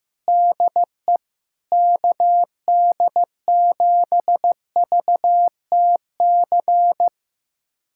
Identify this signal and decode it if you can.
Morse code, heard as a keyed tone, about 15 wpm, 700 Hz. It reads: DE KD7VTC
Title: USA Callsign DE Callsign - 6 Characters - 15wpm